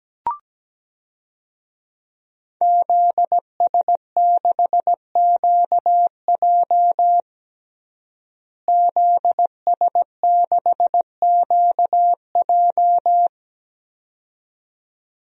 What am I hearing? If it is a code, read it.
Morse code, heard as a keyed tone, about 17 wpm, 700 Hz. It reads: ZS6QJ ZS6QJ
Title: DXCC-100 Call Signs - 5 Characters - Repeated 2x - 17wpm